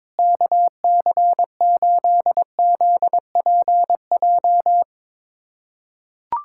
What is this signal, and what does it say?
Morse code, heard as a keyed tone, about 22 wpm, 700 Hz. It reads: KC8ZPJ
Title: USA Callsign DE Callsign - 6 Characters - 22wpm